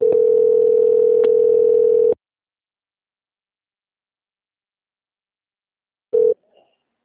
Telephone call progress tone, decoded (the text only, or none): ringback tone